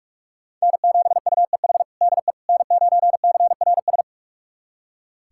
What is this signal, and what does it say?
Morse code, heard as a keyed tone, about 46 wpm, 700 Hz. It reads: N7UEH DE N9CRS
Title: USA Callsign DE Callsign - 5 Characters - 45wpm